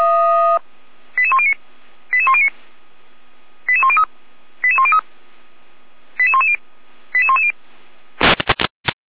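Five-tone selective call (ZVEI), heard as quick runs of five tones, tones of about 70 ms. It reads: 88109 88109 88103 88103 88110 88110
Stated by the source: ZVEI five-tone call 88233 received by SDR.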